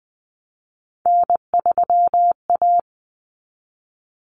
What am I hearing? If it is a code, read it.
Morse code, heard as a keyed tone, about 20 wpm, 700 Hz. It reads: N3A
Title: Common Cut Numbers - 3-Digit 20wpm